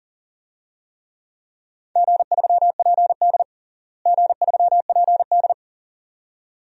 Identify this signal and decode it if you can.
Morse code, heard as a keyed tone, about 37 wpm, 700 Hz. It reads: G3PD G3PD
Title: DXCC-100 Call Signs - 4 Characters - Repeated 2x - 40wpm